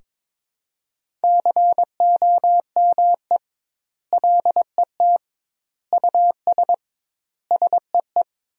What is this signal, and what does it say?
Morse code, heard as a keyed tone, about 21 wpm, 700 Hz. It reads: COME LET US SEE
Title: 2x Word-Spacing - Sentences from Top 300 Words - 22wpm